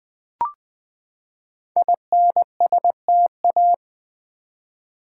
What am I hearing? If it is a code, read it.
Morse code, heard as a keyed tone, about 20 wpm, 700 Hz. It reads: INSTA
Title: Build A Word - 5 Letters At A Time - 12-Letter Words - 20wpm